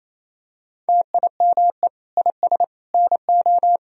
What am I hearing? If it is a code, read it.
Morse code, heard as a keyed tone, about 28 wpm, 700 Hz. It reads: TIME IS NO
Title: Sentences from Top 1000 Words - Review Part 3 - 28wpm